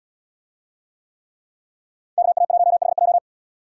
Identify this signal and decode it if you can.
Morse code, heard as a keyed tone, about 77 wpm, 700 Hz. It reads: KA1HJ